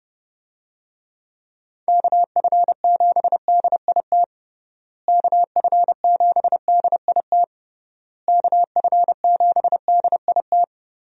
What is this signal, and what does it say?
Morse code, heard as a keyed tone, about 30 wpm, 700 Hz. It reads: KF7DIT KF7DIT KF7DIT
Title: USA Call Signs - 6 Characters - Repeated 3x - 30wpm